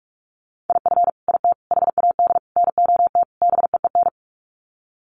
Morse code, heard as a keyed tone, about 44 wpm, 700 Hz. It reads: IF IT HAD NOT BEEN
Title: Sets of 5 Words Encore 45wpm